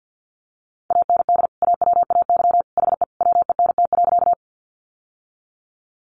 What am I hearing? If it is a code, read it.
Morse code, heard as a keyed tone, about 50 wpm, 700 Hz. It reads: AND AWAY HE WENT.
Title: Sentences from Top 600 Words 50wpm